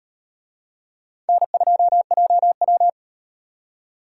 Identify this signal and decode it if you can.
Morse code, heard as a keyed tone, about 38 wpm, 700 Hz. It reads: N2JW